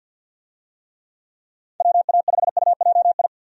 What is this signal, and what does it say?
Morse code, heard as a keyed tone, about 52 wpm, 700 Hz. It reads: WA5UJI
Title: USA Call Signs - 6 Characters - 50wpm